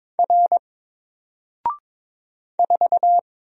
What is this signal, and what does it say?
Morse code, heard as a keyed tone, about 22 wpm, 700 Hz. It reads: R 4
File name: Single-Letter-Number-22wpm